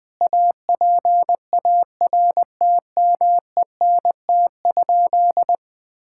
Morse code, heard as a keyed tone, about 20 wpm, 700 Hz. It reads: APARTMENT?
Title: Sentences from Top 1000 Words - Review Part 6 - 20wpm